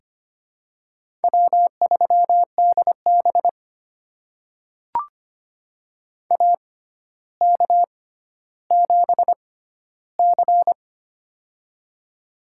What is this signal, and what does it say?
Morse code, heard as a keyed tone, about 25 wpm, spaced as for 13 wpm, 700 Hz. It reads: W3DB AK7C
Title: USA Call Signs - 4 Characters - 6x Character Spacing - 25wpm